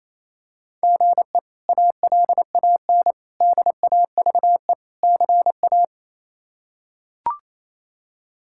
Morse code, heard as a keyed tone, about 27 wpm, 700 Hz. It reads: GE ALAN DAVE CA
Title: K1USN SST QSO Practice - 28wpm